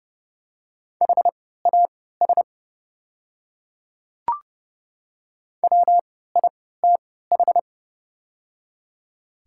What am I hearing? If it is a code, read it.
Morse code, heard as a keyed tone, about 31 wpm, spaced as for 17 wpm, 690 Hz. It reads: HAS WITH
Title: 3x Character Spacing - Top 100 Words - 30wpm